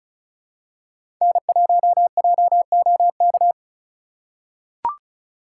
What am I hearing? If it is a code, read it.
Morse code, heard as a keyed tone, about 36 wpm, 700 Hz. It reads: N1JOK
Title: USA Call Signs - 5 Characters - 4x Character Spacing - 35wpm